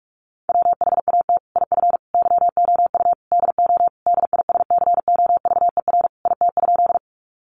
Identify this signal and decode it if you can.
Morse code, heard as a keyed tone, about 44 wpm, 700 Hz. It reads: WHAT IF YOU DO DISCOVER IT?
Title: Sentences from Top 1100 Words 45wpm